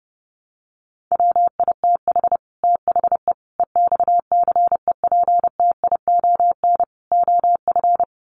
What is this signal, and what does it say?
Morse code, heard as a keyed tone, about 30 wpm, 700 Hz. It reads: WITH THE EXCEPTION OF
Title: Sets of 4 Words 30wpm